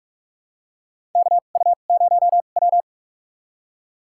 Morse code, heard as a keyed tone, about 45 wpm, spaced as for 32 wpm, 700 Hz. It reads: KU0W
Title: USA Call Signs - 4 Characters - 2x Character Spacing - 45wpm